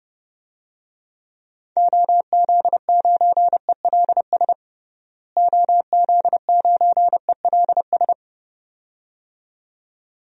Morse code, heard as a keyed tone, about 30 wpm, 700 Hz. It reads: OZ9ELS OZ9ELS
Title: DXCC-100 Call Signs - 6 Characters - Repeated 2x - 30wpm